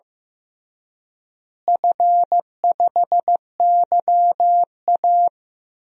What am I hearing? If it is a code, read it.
Morse code, heard as a keyed tone, about 15 wpm, 700 Hz. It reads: F5YA
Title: DXCC-100 Call Signs - 4 Characters - Repeated 2x - 15wpm